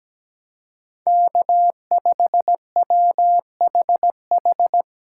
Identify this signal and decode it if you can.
Morse code, heard as a keyed tone, about 20 wpm, 700 Hz. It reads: K5WHH
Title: Speed Racing - USA 1x3 Call Signs 17wpm